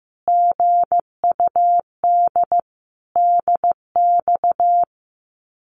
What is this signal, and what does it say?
Morse code, heard as a keyed tone, about 15 wpm, 700 Hz. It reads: GUD DX
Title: Common QSO Phrases Encore 15wpm